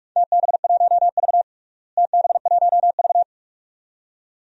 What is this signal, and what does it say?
Morse code, heard as a keyed tone, about 41 wpm, 700 Hz. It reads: TB1V TB1V
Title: DXCC-100 Call Signs - 4 Characters - Repeated 2x - 45wpm